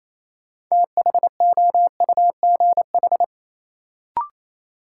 Morse code, heard as a keyed tone, about 28 wpm, 700 Hz. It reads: THOUGH
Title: Sentences from Top 400 Words - New Word Drill - 28wpm